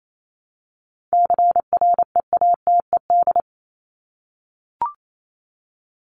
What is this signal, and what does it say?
Morse code, heard as a keyed tone, about 28 wpm, 700 Hz. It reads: CREATED
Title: ED Suffix - Top 100 Words with Base Words 28wpm